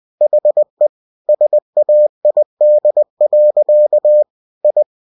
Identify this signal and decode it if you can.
Morse code, heard as a keyed tone, about 20 wpm, 600 Hz. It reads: HE SAID. I
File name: cw-books / winnie-the-pooh / winnie-20wpm-page-0057